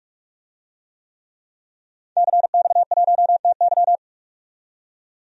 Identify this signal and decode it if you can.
Morse code, heard as a keyed tone, about 42 wpm, 700 Hz. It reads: CX1TY
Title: DXCC-100 Call Signs - 5 Characters - 45wpm